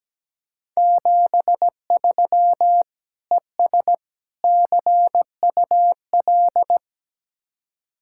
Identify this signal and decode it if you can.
Morse code, heard as a keyed tone, about 18 wpm, 700 Hz. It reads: 73 ES CUL
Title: Speed-Racing Common QSO Phrases Encore - 17wpm